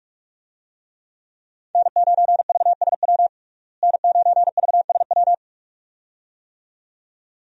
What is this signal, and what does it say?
Morse code, heard as a keyed tone, about 46 wpm, 700 Hz. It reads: N9VSW N9VSW
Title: USA Call Signs - 5 Characters - Repeated 2x - 45wpm